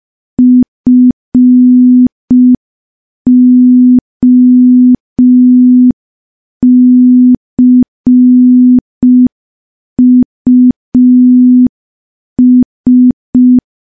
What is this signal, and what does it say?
Morse code, heard as a keyed tone, about 5 wpm, 250 Hz. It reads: FOCUS
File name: morse / morse-training-05WPM